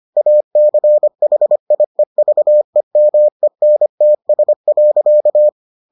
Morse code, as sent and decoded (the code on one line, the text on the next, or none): .- -.-. .... .. ...- . -- . -. - ... .-.-.-
ACHIEVEMENTS.